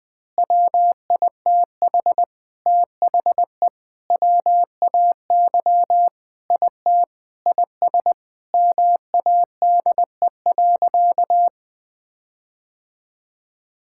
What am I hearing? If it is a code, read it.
Morse code, heard as a keyed tone, about 21 wpm, 700 Hz. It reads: WITH THE WAY IT IS MADE.
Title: Sentences with 5-Word Phrases - 20wpm